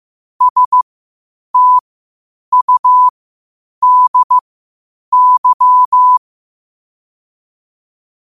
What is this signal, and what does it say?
Morse code, heard as a keyed tone, about 15 wpm, spaced as for 8 wpm, 1000 Hz. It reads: STUDY
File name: qotdfiles / 8wpm